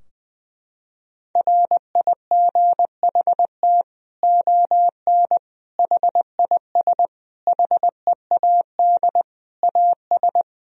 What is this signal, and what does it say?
Morse code, heard as a keyed tone, about 20 wpm, 700 Hz. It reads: RIGHT ON HIS HEAD AS
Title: Sentences from Top 1000 Words - Review Part 2 - 20wpm